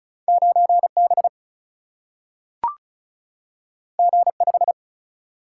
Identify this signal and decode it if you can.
Morse code, heard as a keyed tone, about 36 wpm, 700 Hz. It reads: 9B G5
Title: Single-Letter Single-Number 35wpm